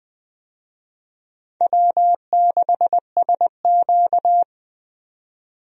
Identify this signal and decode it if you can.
Morse code, heard as a keyed tone, about 20 wpm, 700 Hz. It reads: W6SQ
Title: USA Call Signs - 4 Characters - 20wpm